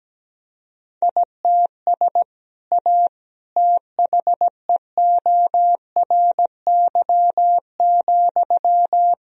Morse code, heard as a keyed tone, about 17 wpm, 700 Hz. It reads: ITS A THEORY,
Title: Sentences from Top 1000 Words - Review Part 3 - 17wpm